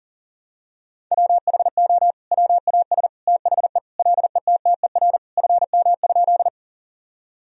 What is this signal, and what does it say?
Morse code, heard as a keyed tone, about 40 wpm, 700 Hz. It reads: WHO WAS THE LETTER FM?
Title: Sentences from Top 300 Words and Abbreviations 40wpm